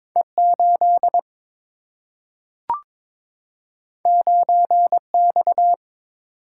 Morse code, heard as a keyed tone, about 22 wpm, 700 Hz. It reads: E 8 9 X
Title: Single-Letter Single-Number 22wpm